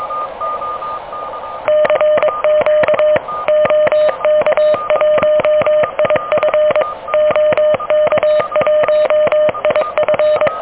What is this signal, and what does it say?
Morse code, heard as a keyed tone, about 24 wpm, 610 Hz. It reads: CQ OK1IF OK1IF
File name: OK QRP 2023 / OK QRP 2023